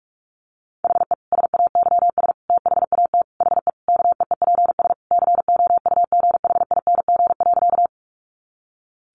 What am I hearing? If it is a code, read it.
Morse code, heard as a keyed tone, about 44 wpm, 700 Hz. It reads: HE SAYS THAT HE KEEPS COUGHING.
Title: Additional Easy Phrases 45wpm